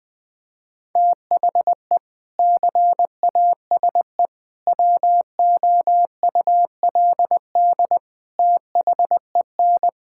Morse code, heard as a keyed tone, about 21 wpm, 700 Hz. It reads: THE CASE WOULD THEN
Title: Sentences from Top 1600 Words 20wpm